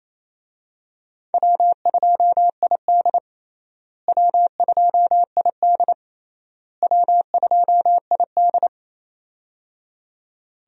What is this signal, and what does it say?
Morse code, heard as a keyed tone, about 28 wpm, 700 Hz. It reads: W2ID W2ID W2ID
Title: USA Call Signs - 4 Characters - Repeated 3x - 28wpm